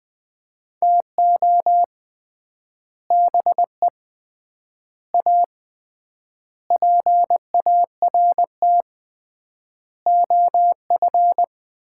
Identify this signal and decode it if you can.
Morse code, heard as a keyed tone, about 20 wpm, 700 Hz. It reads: TO BE A PART OF
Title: 3x Word-Spacing - Sets of 5 Words - 20wpm